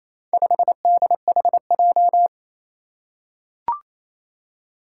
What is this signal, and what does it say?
Morse code, heard as a keyed tone, about 28 wpm, 700 Hz. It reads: 5DHJ
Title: Four Challenging Letter-Number Combinations 28wpm